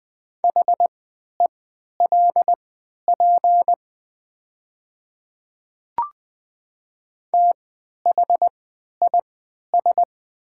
Morse code, heard as a keyed tone, about 20 wpm, spaced as for 11 wpm, 700 Hz. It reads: HELP THIS
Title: 3x Character Spacing - Top 300 Words - 20wpm